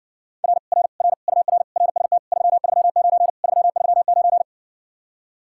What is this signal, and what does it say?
Morse code, heard as a keyed tone, about 70 wpm, 700 Hz. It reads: R R R UR RST 339 339